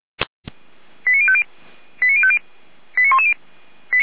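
Five-tone selective call (ZVEI), heard as quick runs of five tones, tones of about 70 ms. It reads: 89050 89050 89110
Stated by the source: ZVEI five-tone selective call received by SDR, from the Feuerwehr Orlamünde audio server.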